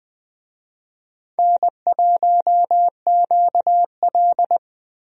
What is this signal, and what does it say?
Morse code, heard as a keyed tone, about 21 wpm, 700 Hz. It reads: N1QL
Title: USA Call Signs - 4 Characters - Repeated 2x - 20wpm